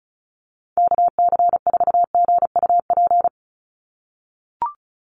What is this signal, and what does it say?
Morse code, heard as a keyed tone, about 34 wpm, 700 Hz. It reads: KC4GUP
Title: USA Call Signs 35wpm